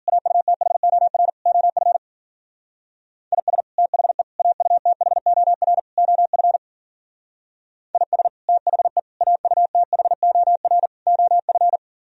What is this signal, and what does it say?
Morse code, heard as a keyed tone, about 50 wpm, 700 Hz. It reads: AUTHOR OF IS THE AUTHOR OF IS THE AUTHOR OF